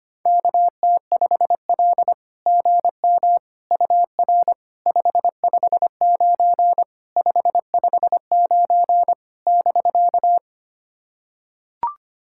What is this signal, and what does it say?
Morse code, heard as a keyed tone, about 25 wpm, 700 Hz. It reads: KT5L GM UR 559 559 <BK>
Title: SOTA QSO Practice 25wpm